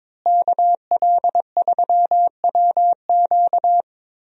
Morse code, heard as a keyed tone, about 22 wpm, 700 Hz. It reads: KL3WQ
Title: USA Call Signs - 5 Characters - Repeated 3x - 22wpm